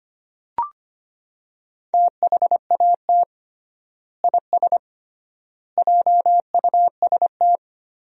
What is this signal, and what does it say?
Morse code, heard as a keyed tone, about 25 wpm, 700 Hz. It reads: THAT IS JUST
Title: 3x Word-Spacing - Sentences from Top 200 Words - 25wpm